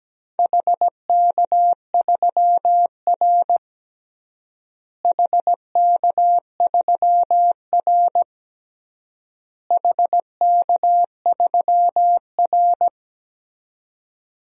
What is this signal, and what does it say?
Morse code, heard as a keyed tone, about 17 wpm, 700 Hz. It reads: HK3R HK3R HK3R
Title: DXCC-100 Call Signs - 4 Characters - Repeated 3x - 17wpm